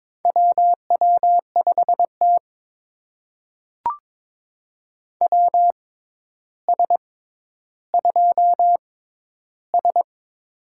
Morse code, heard as a keyed tone, about 22 wpm, spaced as for 11 wpm, 700 Hz. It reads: WW5T WS2S
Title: USA Call Signs - 4 Characters - 6x Character Spacing - 22wpm